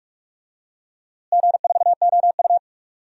Morse code, heard as a keyed tone, about 41 wpm, 700 Hz. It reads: G4OU
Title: DXCC-100 Call Signs - 4 Characters - Repeated 3x - 45wpm